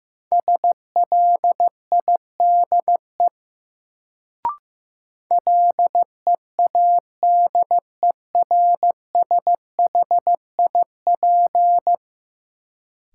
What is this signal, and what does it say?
Morse code, heard as a keyed tone, about 15 wpm, 700 Hz. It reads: SLIDE LEADERSHIP
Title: Sentences from Top 1600 Words - New Word Drill - 15wpm